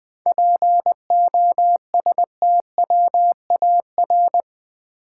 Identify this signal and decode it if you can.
Morse code, heard as a keyed tone, about 20 wpm, 700 Hz. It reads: POSTWAR
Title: 7 Letter Words - First 3 Letters Given - 20wpm